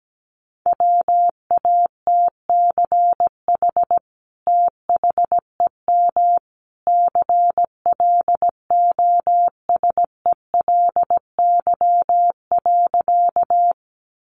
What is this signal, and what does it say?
Morse code, heard as a keyed tone, about 17 wpm, 700 Hz. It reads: WATCH THEM CLOSELY.
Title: Sentences from Top 1400 Words 17wpm